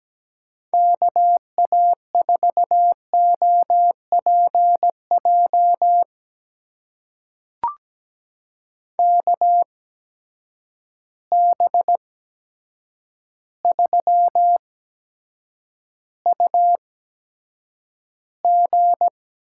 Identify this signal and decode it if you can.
Morse code, heard as a keyed tone, about 17 wpm, spaced as for 8 wpm, 700 Hz. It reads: KA4OPJ KB3UG
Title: USA Call Signs - 8x Character Spacing - 17wpm